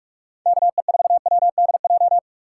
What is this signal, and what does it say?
Morse code, heard as a keyed tone, about 44 wpm, 700 Hz. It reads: KE4WDJ